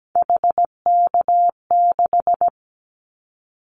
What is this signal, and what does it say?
Morse code, heard as a keyed tone, about 17 wpm, 700 Hz. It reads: HK6
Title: Callsign Prefixes for Top 100 DXCC - 17wpm